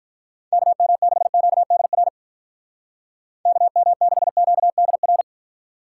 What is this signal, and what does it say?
Morse code, heard as a keyed tone, about 43 wpm, 700 Hz. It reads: KM6QDR KM6QDR
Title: Speed Racing - USA 2x3 Call Signs 35wpm